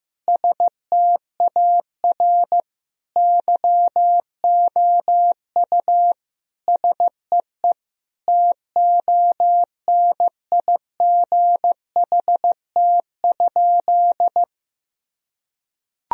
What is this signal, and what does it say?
Morse code, heard as a keyed tone, about 15 wpm, 700 Hz. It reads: STAR YOU SEE TONIGHT?
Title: Sentences from Top 1000 Words - Review Part 2 - 15wpm